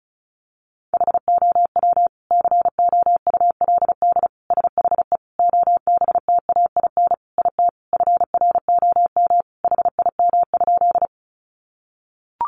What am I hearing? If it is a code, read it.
Morse code, heard as a keyed tone, about 34 wpm, 700 Hz. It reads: HOW COULD SHE OBTAIN IT FROM HIM?
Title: Sentences from Top 1300 Words 35wpm